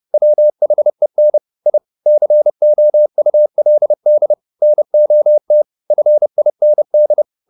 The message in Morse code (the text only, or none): WHEN I COULD NOT FIND